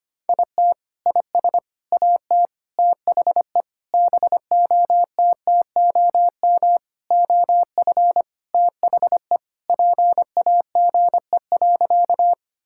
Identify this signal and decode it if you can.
Morse code, heard as a keyed tone, about 25 wpm, 700 Hz. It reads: IT IS AT THE BOTTOM OF THE PAGE.
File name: practice / 25wpm / Sentences with 4-Word Phrases - 25wpm